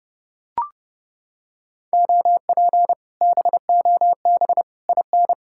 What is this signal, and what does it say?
Morse code, heard as a keyed tone, about 28 wpm, 700 Hz. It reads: OP BOB IN